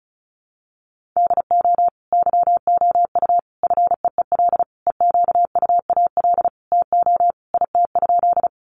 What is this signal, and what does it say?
Morse code, heard as a keyed tone, about 34 wpm, 700 Hz. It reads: DO YOU FEEL EQUAL TO IT?
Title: Sentences from Top 1400 Words 35wpm